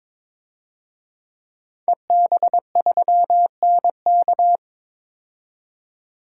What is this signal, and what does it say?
Morse code, heard as a keyed tone, about 22 wpm, 700 Hz. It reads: EB3NK
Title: DXCC-100 Call Signs - 5 Characters - 22wpm